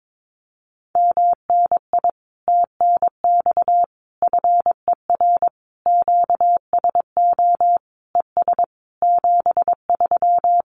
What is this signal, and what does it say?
Morse code, heard as a keyed tone, about 22 wpm, 700 Hz. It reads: MNI TNX FER QSO ES 73
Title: Common QSO Phrases Encore 22wpm